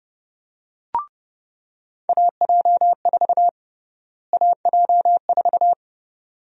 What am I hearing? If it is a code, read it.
Morse code, heard as a keyed tone, about 30 wpm, 700 Hz. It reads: AJ4 AJ4
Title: USA Call Sign Prefixes - Repeated 2x - 30wpm